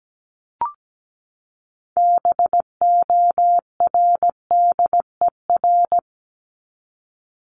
Morse code, heard as a keyed tone, about 17 wpm, 700 Hz. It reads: BORDER
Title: Top 1751-2000 Words 17wpm